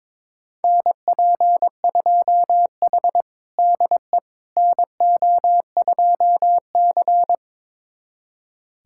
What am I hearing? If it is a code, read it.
Morse code, heard as a keyed tone, about 22 wpm, 700 Hz. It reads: NP2H DE NO2C
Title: USA Callsign DE Callsign - 4 Characters - 22wpm